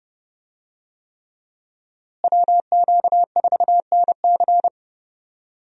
Morse code, heard as a keyed tone, about 32 wpm, 700 Hz. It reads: WQ4NC
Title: USA Call Signs - 5 Characters - Repeated 3x - 30wpm